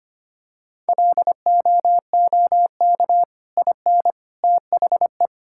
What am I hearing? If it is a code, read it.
Morse code, heard as a keyed tone, about 25 wpm, 700 Hz. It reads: LOOK IN THE
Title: Sentences from Top 1600 Words 25wpm